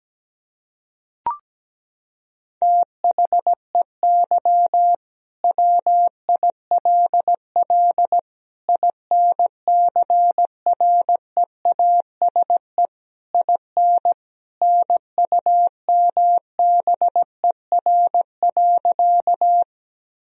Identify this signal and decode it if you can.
Morse code, heard as a keyed tone, about 17 wpm, 700 Hz. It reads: THEY WILL INCREASE IN NUMBER.
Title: Sentences from Top 1000 Words - Review Part 4 - 17wpm